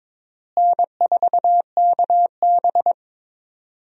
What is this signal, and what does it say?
Morse code, heard as a keyed tone, about 22 wpm, 700 Hz. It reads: N4KB
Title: USA Call Signs - 4 Characters - Repeated 2x - 22wpm